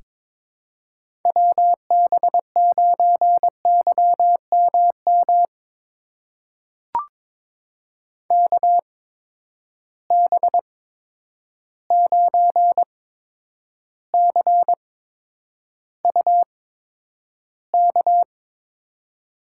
Morse code, heard as a keyed tone, about 22 wpm, spaced as for 9 wpm, 700 Hz. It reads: WB9YMM KB9CUK
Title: USA Call Signs - 6 Characters - 8x Character Spacing - 22wpm